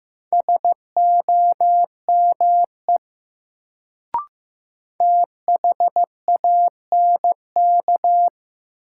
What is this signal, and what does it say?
Morse code, heard as a keyed tone, about 15 wpm, 700 Hz. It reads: SOME THANK
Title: Sets of 2 Words - New Word Drill - 15wpm